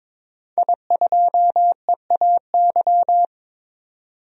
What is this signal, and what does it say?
Morse code, heard as a keyed tone, about 22 wpm, 700 Hz. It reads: I2EAY
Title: DXCC-100 Call Signs - 5 Characters - 22wpm